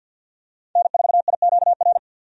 Morse code, heard as a keyed tone, about 50 wpm, 700 Hz. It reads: N4IQR
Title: USA Call Signs - 5 Characters - 50wpm